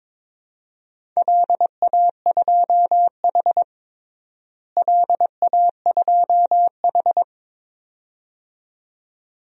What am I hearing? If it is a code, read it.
Morse code, heard as a keyed tone, about 22 wpm, 700 Hz. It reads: LA2H LA2H
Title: DXCC-100 Call Signs - 4 Characters - Repeated 2x - 22wpm